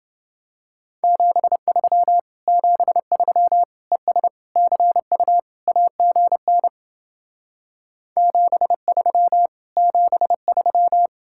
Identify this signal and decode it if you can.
Morse code, heard as a keyed tone, about 28 wpm, 700 Hz. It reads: 73 73 ES CU AGN 73 73